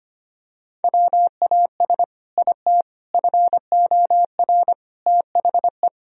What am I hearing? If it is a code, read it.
Morse code, heard as a keyed tone, about 25 wpm, 700 Hz. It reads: WAS IT FOR THE
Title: Sentences from Top 1000 Words - Review Part 3 - 25wpm